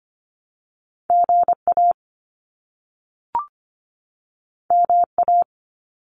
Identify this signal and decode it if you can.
Morse code, heard as a keyed tone, about 25 wpm, 700 Hz. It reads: GA MA
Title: US States 25wpm